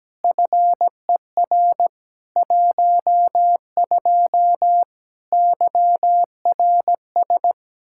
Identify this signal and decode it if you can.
Morse code, heard as a keyed tone, about 17 wpm, 700 Hz. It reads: FER 12 YRS
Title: Practice QSO - Exchange 3 - More Info - 17wpm